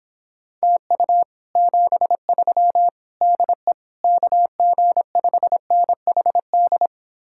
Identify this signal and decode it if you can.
Morse code, heard as a keyed tone, about 26 wpm, 700 Hz. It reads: TU 73 DE KG5NHD